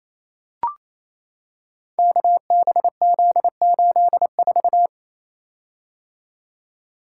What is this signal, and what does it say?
Morse code, heard as a keyed tone, about 29 wpm, 700 Hz. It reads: KBZ84
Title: Code Groups - 5 Letter:Numbers - 28wpm